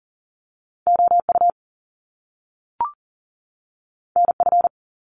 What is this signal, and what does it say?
Morse code, heard as a keyed tone, about 42 wpm, 700 Hz. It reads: OU NF
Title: Top Two Letter Combinations in Words 40wpm